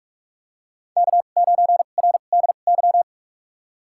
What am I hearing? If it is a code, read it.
Morse code, heard as a keyed tone, about 46 wpm, spaced as for 33 wpm, 700 Hz. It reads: K9RDY
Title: USA Call Signs - 5 Characters - 2x Character Spacing - 45wpm